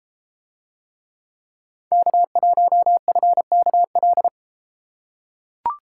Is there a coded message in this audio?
Yes: Morse code from 1.9 to 4.3 s, heard as a keyed tone, about 32 wpm, 700 Hz. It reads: K1FKL